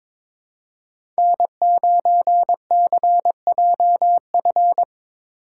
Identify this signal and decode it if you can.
Morse code, heard as a keyed tone, about 22 wpm, 700 Hz. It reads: N9CJF
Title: USA Call Signs - 5 Characters - 4x Character Spacing - 22wpm